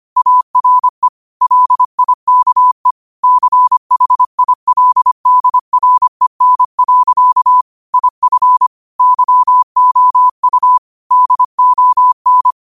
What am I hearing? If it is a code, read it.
Morse code, heard as a keyed tone, about 25 wpm, 1000 Hz. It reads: ARE LIKE CHILDREN. IF YOU DON